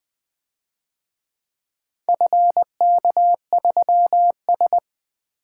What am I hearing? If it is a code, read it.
Morse code, heard as a keyed tone, about 23 wpm, 700 Hz. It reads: FK3S
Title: Speed-Racing - Four Letter-Number Combinations - 20wpm